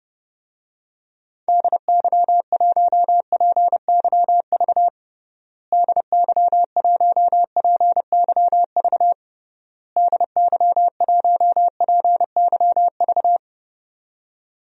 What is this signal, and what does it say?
Morse code, heard as a keyed tone, about 30 wpm, 700 Hz. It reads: DY1PYV DY1PYV DY1PYV
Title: DXCC-100 Call Signs - 6 Characters - Repeated 3x - 30wpm